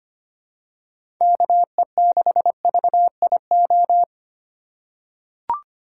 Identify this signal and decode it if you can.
Morse code, heard as a keyed tone, about 25 wpm, 700 Hz. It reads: KE6VIO
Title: USA Call Signs - 6 Characters - 4x Character Spacing - 25wpm